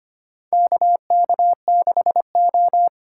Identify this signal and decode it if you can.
Morse code, heard as a keyed tone, about 25 wpm, 690 Hz. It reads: KK6O
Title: USA Call Signs - 4 Characters - 4x Character Spacing - 25wpm